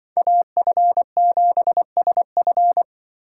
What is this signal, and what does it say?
Morse code, heard as a keyed tone, about 25 wpm, 700 Hz. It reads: AF7SF
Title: Speed Racing - USA 2x2 Call Signs 20wpm